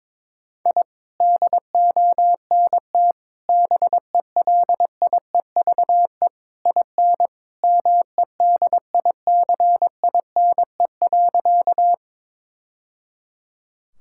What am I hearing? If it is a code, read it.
Morse code, heard as a keyed tone, about 22 wpm, 700 Hz. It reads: I DONT BELIEVE IN MEDICINE.
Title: Sentences from Top 1000 Words - Review Part 6 - 22wpm